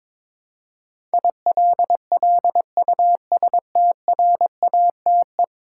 Morse code, heard as a keyed tone, about 22 wpm, 700 Hz. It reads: ILLUSTRATE
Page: morse-code-ninja-practice-22wpm